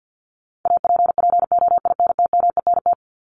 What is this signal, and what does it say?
Morse code, heard as a keyed tone, about 50 wpm, 700 Hz. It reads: APPOINTMENT
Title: Top 1751-2000 Words 50wpm